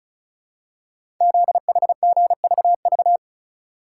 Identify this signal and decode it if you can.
Morse code, heard as a keyed tone, about 36 wpm, 700 Hz. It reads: ZHGVV
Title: Code Groups - 5 Letters - 35wpm